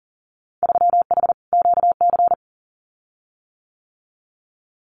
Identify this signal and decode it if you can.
Morse code, heard as a keyed tone, about 40 wpm, 700 Hz. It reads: 3H QC
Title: Winter Field Day Exchanges 40wpm